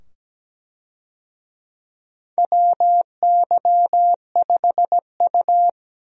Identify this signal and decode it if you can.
Morse code, heard as a keyed tone, about 17 wpm, 700 Hz. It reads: WY5U
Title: USA Call Signs - 4 Characters - 17wpm